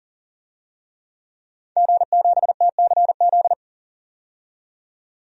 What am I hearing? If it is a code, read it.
Morse code, heard as a keyed tone, about 40 wpm, 700 Hz. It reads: G7TCZ
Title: DXCC-100 Call Signs - 5 Characters - 40wpm